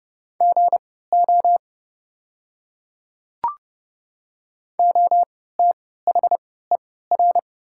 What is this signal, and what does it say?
Morse code, heard as a keyed tone, about 30 wpm, spaced as for 17 wpm, 700 Hz. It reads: GO OTHER
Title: 3x Character Spacing - Top 100 Words - 30wpm